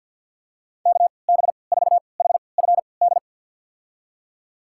Morse code, heard as a keyed tone, about 46 wpm, spaced as for 27 wpm, 690 Hz. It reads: KB4HFD
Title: USA Call Signs - 3x Character Spacing - 50wpm